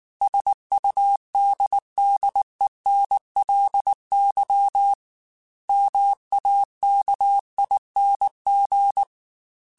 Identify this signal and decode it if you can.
Morse code, heard as a keyed tone, about 19 wpm, 800 Hz. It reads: SUDDENLY MAKING
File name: Technical Subjects / Technical Subjects-mp3s / AND Gate @18